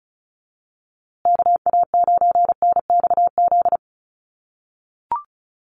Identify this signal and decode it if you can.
Morse code, heard as a keyed tone, about 34 wpm, 700 Hz. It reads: KA9NXZ
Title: USA Call Signs Repeated 2x 35wpm